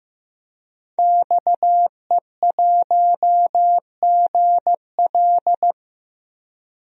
Morse code, heard as a keyed tone, about 15 wpm, 700 Hz. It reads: XE1GL
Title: DXCC-100 Call Signs - 5 Characters - 15wpm